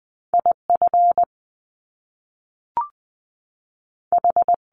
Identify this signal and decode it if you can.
Morse code, heard as a keyed tone, about 20 wpm, 700 Hz. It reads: IF H
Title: Top Two Letter Combinations in Words 20wpm